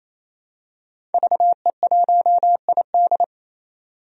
Morse code, heard as a keyed tone, about 28 wpm, 700 Hz. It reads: VE1ID
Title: DXCC-100 Call Signs - 5 Characters - 28wpm